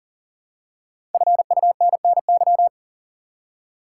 Morse code, heard as a keyed tone, about 40 wpm, 700 Hz. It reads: FUNNY
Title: Five Letter Words - 40wpm